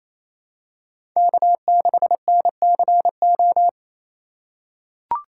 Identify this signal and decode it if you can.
Morse code, heard as a keyed tone, about 27 wpm, 700 Hz. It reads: K6NCO
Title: USA Call Signs - 5 Characters - 4x Character Spacing - 28wpm